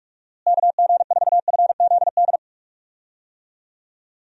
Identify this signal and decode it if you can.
Morse code, heard as a keyed tone, about 46 wpm, 700 Hz. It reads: KG4FZD
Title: USA Call Signs - 6 Characters - Repeated 3x - 45wpm